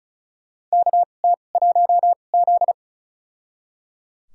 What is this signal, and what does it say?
Morse code, heard as a keyed tone, about 36 wpm, spaced as for 25 wpm, 700 Hz. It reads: KT1Z
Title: USA Call Signs - 4 Characters - 2x Character Spacing - 35wpm